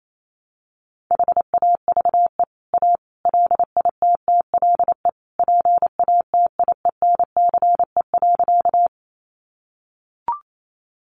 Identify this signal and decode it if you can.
Morse code, heard as a keyed tone, about 28 wpm, 700 Hz. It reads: HAVE A LITTLE PATIENCE.
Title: Sentences from Top 1300 Words 28wpm